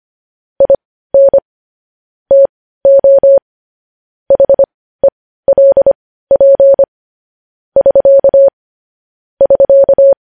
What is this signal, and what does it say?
Morse code, heard as a keyed tone, about 25 wpm, spaced as for 15 wpm, 550 Hz. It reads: IN TO HELP <SK> <SK>
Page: News Headlines in Morse Code at 15 WPM